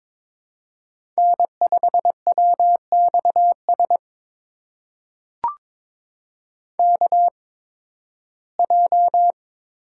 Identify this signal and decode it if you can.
Morse code, heard as a keyed tone, about 22 wpm, spaced as for 14 wpm, 700 Hz. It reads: N5WXS KJ
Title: USA Call Signs - 5 Characters - 8x Character Spacing - 22wpm